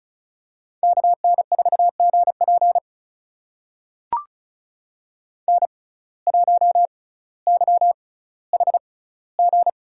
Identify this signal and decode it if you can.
Morse code, heard as a keyed tone, about 36 wpm, spaced as for 18 wpm, 700 Hz. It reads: KN4GP N1YHG
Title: USA Call Signs - 5 Characters - 6x Character Spacing - 35wpm